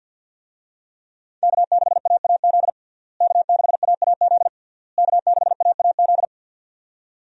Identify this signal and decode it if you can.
Morse code, heard as a keyed tone, about 46 wpm, 700 Hz. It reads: K6AAZ K6AAZ K6AAZ
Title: USA Call Signs - 5 Characters - Repeated 3x - 50wpm